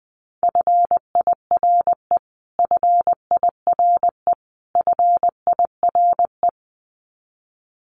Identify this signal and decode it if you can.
Morse code, heard as a keyed tone, about 19 wpm, 700 Hz. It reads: FIRE FIRE FIRE
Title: Top 500 Words Repeated 20wpm